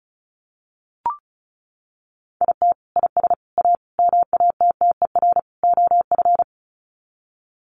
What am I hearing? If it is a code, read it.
Morse code, heard as a keyed tone, about 34 wpm, 700 Hz. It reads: IT IS A MATTER OF
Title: Sets of 5 Words Encore2 35wpm